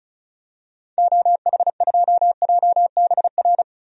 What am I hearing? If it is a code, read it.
Morse code, heard as a keyed tone, about 35 wpm, 700 Hz. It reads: OH2JBR
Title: DXCC-100 Call Signs - 6 Characters - 35wpm